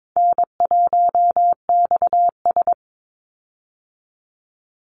Morse code, heard as a keyed tone, about 22 wpm, 700 Hz. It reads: N1XS
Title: Easy Call Signs Repeated 3x - 22wpm